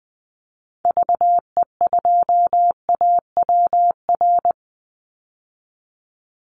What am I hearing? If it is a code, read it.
Morse code, heard as a keyed tone, about 20 wpm, 700 Hz. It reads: VE2AWR
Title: Call Signs Repeated 3x - 20wpm